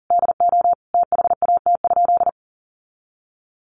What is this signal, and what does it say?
Morse code, heard as a keyed tone, about 40 wpm, 700 Hz. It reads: DO THAT?
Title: Sentences from Top 900 Words 40wpm